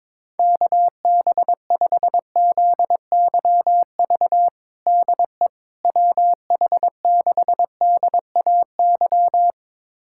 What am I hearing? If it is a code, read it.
Morse code, heard as a keyed tone, about 22 wpm, 700 Hz. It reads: KB5ZYV DE WH6DAY
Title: USA Callsign DE Callsign - 6 Characters - 22wpm